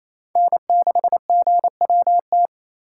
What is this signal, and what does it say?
Morse code, heard as a keyed tone, about 28 wpm, 700 Hz. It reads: N6GWT
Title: USA Call Signs - 5 Characters - 2x Character Spacing - 28wpm